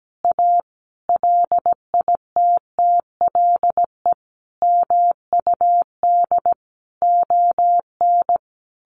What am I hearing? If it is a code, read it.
Morse code, heard as a keyed tone, about 17 wpm, 700 Hz. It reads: A LITTLE MUD ON
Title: A Sound Of Thunder 17wpm